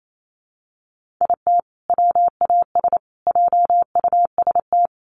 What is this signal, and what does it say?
Morse code, heard as a keyed tone, about 28 wpm, 700 Hz. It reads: IT WAS JUST
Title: Sets of 3 Words 28wpm